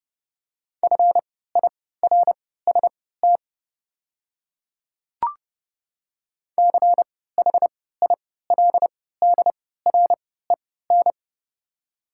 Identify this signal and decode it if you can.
Morse code, heard as a keyed tone, about 30 wpm, spaced as for 17 wpm, 700 Hz. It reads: FIRST CHILDREN
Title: 3x Character Spacing - Top 300 Words - 30wpm